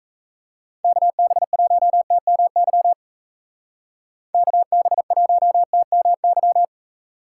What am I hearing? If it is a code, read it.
Morse code, heard as a keyed tone, about 34 wpm, 700 Hz. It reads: KB1TMY KB1TMY
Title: Speed Racing - USA 2x3 Call Signs 28wpm